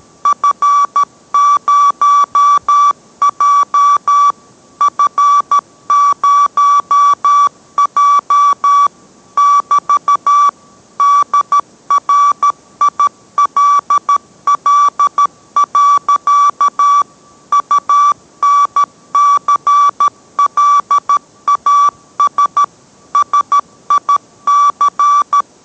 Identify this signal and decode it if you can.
Morse code, heard as a keyed tone, about 13 wpm, 1150 Hz. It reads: F0J F0J <BT> DRILL. UNCLAS SIC